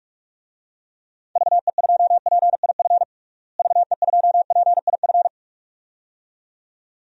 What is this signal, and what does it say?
Morse code, heard as a keyed tone, about 46 wpm, 700 Hz. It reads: VE2PIF VE2PIF
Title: DXCC-100 Call Signs - 6 Characters - Repeated 2x - 45wpm